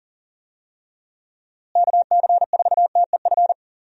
Morse code, heard as a keyed tone, about 39 wpm, 700 Hz. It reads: KC4TEF
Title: USA Call Signs - 6 Characters - Repeated 2x - 40wpm